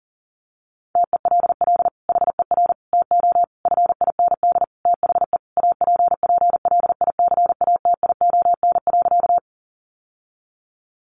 Morse code, heard as a keyed tone, about 40 wpm, 700 Hz. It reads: TELL HER TO FIND THE APPLICATION.
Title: Sentences from Top 1200 Words 40wpm